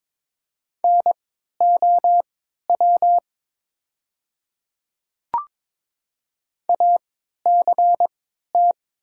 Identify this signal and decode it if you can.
Morse code, heard as a keyed tone, about 22 wpm, spaced as for 13 wpm, 700 Hz. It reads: NOW ACT
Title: 3x Character Spacing - Top 200 Words - 22wpm